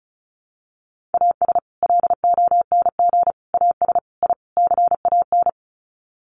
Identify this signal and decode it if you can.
Morse code, heard as a keyed tone, about 34 wpm, 700 Hz. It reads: AS LONG AS I CAN
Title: Sets of 5 Words Encore2 35wpm